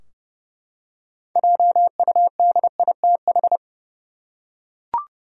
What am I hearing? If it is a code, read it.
Morse code, heard as a keyed tone, about 30 wpm, 700 Hz. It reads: JUDITH
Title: USA Common Names - 30wpm